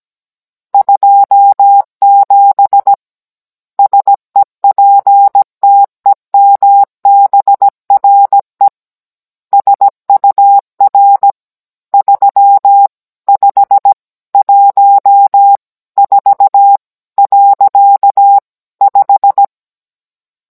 Morse code, heard as a keyed tone, about 17 wpm, 800 Hz. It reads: 27 SEPTEMBRE SUR 3514.5